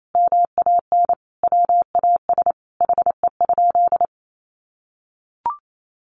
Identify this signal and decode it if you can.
Morse code, heard as a keyed tone, about 28 wpm, 700 Hz. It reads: MAN WAS HE?
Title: Sentences from Top 1300 Words 28wpm